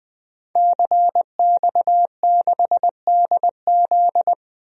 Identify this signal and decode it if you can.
Morse code, heard as a keyed tone, about 20 wpm, 700 Hz. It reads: CX6DZ
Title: DXCC-100 Call Signs - 5 Characters - Repeated 3x - 20wpm